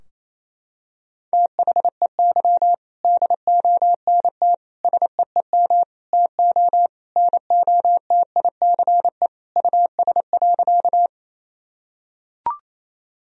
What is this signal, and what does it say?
Morse code, heard as a keyed tone, about 27 wpm, 700 Hz. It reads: THEY DONT SEEM TO NOTICE US.
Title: Sentences from Top 1000 Words - Review Part 2 - 28wpm